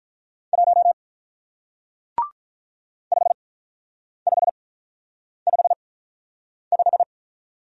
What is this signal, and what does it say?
Morse code, heard as a keyed tone, about 45 wpm, spaced as for 9 wpm, 700 Hz. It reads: 1 5555